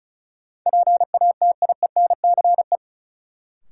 Morse code, heard as a keyed tone, about 36 wpm, 700 Hz. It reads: PATIENCE
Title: Sentences from Top 1300 Words - New Word Drill Alt - 35wpm